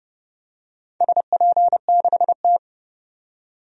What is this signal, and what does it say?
Morse code, heard as a keyed tone, about 29 wpm, 700 Hz. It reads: SP6T
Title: DXCC-100 Call Signs - 4 Characters - 30wpm